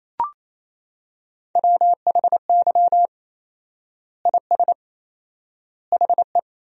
Morse code, heard as a keyed tone, about 28 wpm, 700 Hz. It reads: WHY IS HE
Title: 4x Word-Spacing - Sentences from Top 200 Words - 28wpm